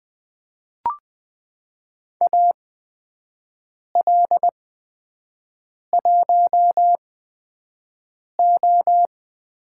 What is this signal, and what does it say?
Morse code, heard as a keyed tone, about 20 wpm, spaced as for 5 wpm, 700 Hz. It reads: AL1O
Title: USA Call Signs - 4 Characters - 8x Character Spacing - 20wpm